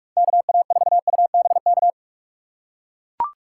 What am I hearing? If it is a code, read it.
Morse code, heard as a keyed tone, about 44 wpm, 700 Hz. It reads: KA4UBK